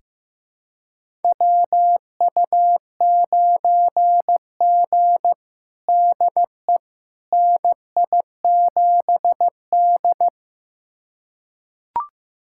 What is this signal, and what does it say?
Morse code, heard as a keyed tone, about 15 wpm, 700 Hz. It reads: WU9G DE NI7D
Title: USA Callsign DE Callsign - 4 Characters - 15wpm